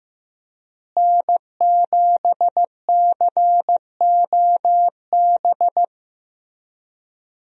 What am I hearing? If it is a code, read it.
Morse code, heard as a keyed tone, about 15 wpm, 700 Hz. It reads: N7COB
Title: USA Call Signs - 5 Characters - Repeated 3x - 15wpm